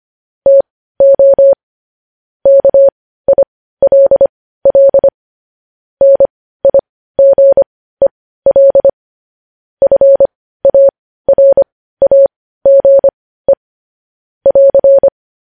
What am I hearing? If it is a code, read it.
Morse code, heard as a keyed tone, about 25 wpm, spaced as for 15 wpm, 550 Hz. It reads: TO KILL NIGEL FARAGE <AR>
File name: podcast / News headlines-1x-25x15